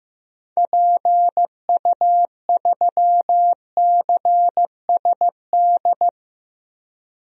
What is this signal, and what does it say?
Morse code, heard as a keyed tone, about 16 wpm, 700 Hz. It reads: PU3CSD
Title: DXCC-100 Call Signs - 6 Characters - Repeated 2x - 15wpm